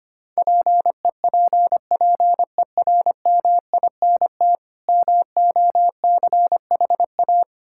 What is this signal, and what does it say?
Morse code, heard as a keyed tone, about 25 wpm, 700 Hz. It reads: PEPPERMINT MOCHA
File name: practice / 25wpm / Difficult Words with Hints - 25wpm